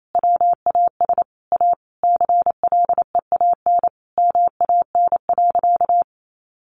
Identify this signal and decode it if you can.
Morse code, heard as a keyed tone, about 28 wpm, 700 Hz. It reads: WAS A CLEAN MAN.
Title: Sentences from Top 1400 Words 28wpm